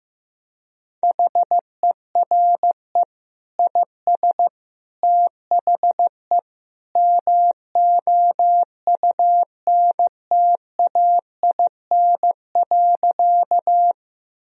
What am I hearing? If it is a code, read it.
Morse code, heard as a keyed tone, about 15 wpm, 700 Hz. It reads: HERE IS THE MOUNTAIN.
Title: Sentences from Top 1000 Words - Review Part 1 - 15wpm